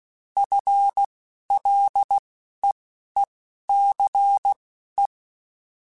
Morse code, heard as a keyed tone, about 16 wpm, spaced as for 11 wpm, 800 Hz. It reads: FLEECE